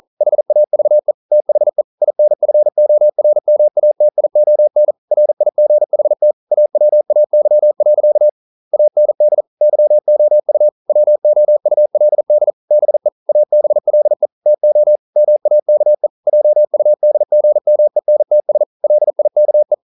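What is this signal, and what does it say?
Morse code, heard as a keyed tone, about 42 wpm, 600 Hz. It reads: HAVE THE INFORMATION RIGHT AWAY. AND YOU WOULD BE ABLE TO MAKE JUDGMENTS LIKE
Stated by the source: Morse code practice at 41 wpm, from CW med Gnister.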